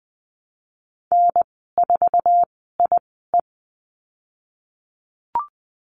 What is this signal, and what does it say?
Morse code, heard as a keyed tone, about 20 wpm, spaced as for 15 wpm, 700 Hz. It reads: N4IE
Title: 055 Lesson 10g - Letter r - Call Signs 20wpm